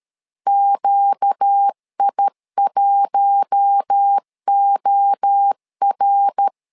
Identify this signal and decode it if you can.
Morse code, heard as a keyed tone, about 13 wpm, 800 Hz. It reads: QI1OR
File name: LEZIONI / 4 Q1 12wpm 2028